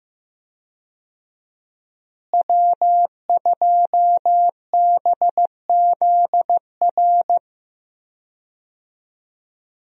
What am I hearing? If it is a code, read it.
Morse code, heard as a keyed tone, about 15 wpm, 700 Hz. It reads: W2BZR
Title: USA Call Signs - 5 Characters - Repeated 3x - 15wpm